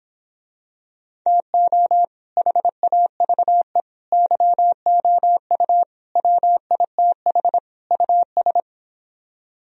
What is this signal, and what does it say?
Morse code, heard as a keyed tone, about 26 wpm, 700 Hz. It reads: TO HAVE YOU WITH US
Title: Speed-Racing - Sets of 5 Words 22wpm